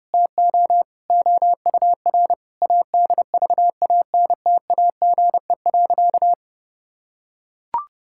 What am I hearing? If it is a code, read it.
Morse code, heard as a keyed tone, about 30 wpm, 700 Hz. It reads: TO OUR ADVANTAGE.
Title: Sentences from Top 1000 Words - Review Part 6 - 30wpm